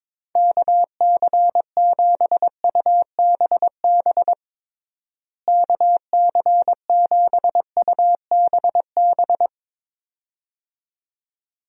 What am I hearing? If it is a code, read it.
Morse code, heard as a keyed tone, about 22 wpm, 700 Hz. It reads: KC7UBB KC7UBB
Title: USA Call Signs - 6 Characters - Repeated 2x - 22wpm